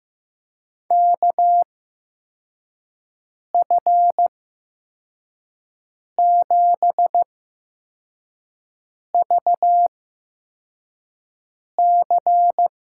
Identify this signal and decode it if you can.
Morse code, heard as a keyed tone, about 15 wpm, spaced as for 4 wpm, 700 Hz. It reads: KF7VC
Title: USA Call Signs - 6 Characters - 8x Character Spacing - 15wpm